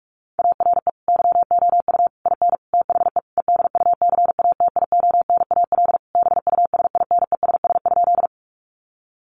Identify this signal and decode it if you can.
Morse code, heard as a keyed tone, about 44 wpm, 700 Hz. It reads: ARE YOU IN THE EDUCATIONAL BUSINESS?
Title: Sentences from Top 1200 Words 45wpm